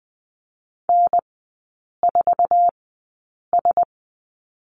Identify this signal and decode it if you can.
Morse code, heard as a keyed tone, about 20 wpm, spaced as for 8 wpm, 690 Hz. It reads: N4S
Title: 060 Lesson 11e - Letter h - Three Characters 20wpm